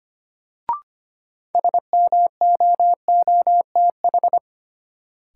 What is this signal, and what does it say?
Morse code, heard as a keyed tone, about 25 wpm, 700 Hz. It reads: SMOOTH
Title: Sentences from Top 1400 Words - New Word Drill Alt - 25wpm